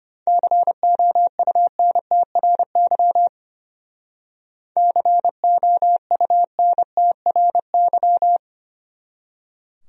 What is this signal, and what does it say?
Morse code, heard as a keyed tone, about 30 wpm, 700 Hz. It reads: COUNTRY COUNTRY
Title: Speed-Racing Top 300 Words 25wpm